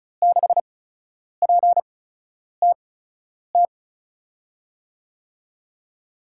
Morse code, heard as a keyed tone, about 34 wpm, spaced as for 10 wpm, 700 Hz. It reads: 6PTT
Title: USA Call Signs - 5 Characters - 8x Character Spacing - 35wpm